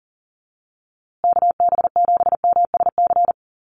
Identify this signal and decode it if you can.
Morse code, heard as a keyed tone, about 40 wpm, 700 Hz. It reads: KB7MSC
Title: USA Call Signs Repeated 2x 40wpm